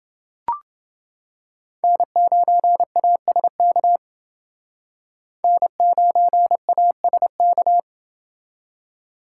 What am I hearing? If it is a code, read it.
Morse code, heard as a keyed tone, about 28 wpm, 700 Hz. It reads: N9ASK N9ASK